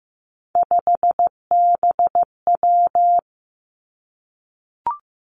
Morse code, heard as a keyed tone, about 15 wpm, 700 Hz. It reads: 5BW
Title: Three Challenging Letter-Number Combinations 15wpm